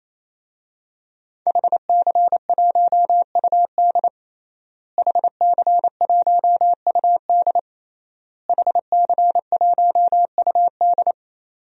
Morse code, heard as a keyed tone, about 28 wpm, 700 Hz. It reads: HC1UD HC1UD HC1UD
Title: DXCC-100 Call Signs - 5 Characters - Repeated 3x - 28wpm